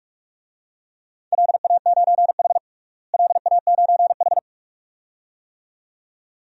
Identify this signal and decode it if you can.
Morse code, heard as a keyed tone, about 46 wpm, 700 Hz. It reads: LA9H LA9H
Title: DXCC-100 Call Signs - 4 Characters - Repeated 2x - 45wpm